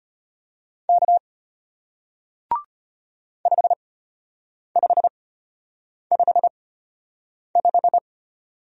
Morse code, heard as a keyed tone, about 33 wpm, spaced as for 8 wpm, 700 Hz. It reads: K 5555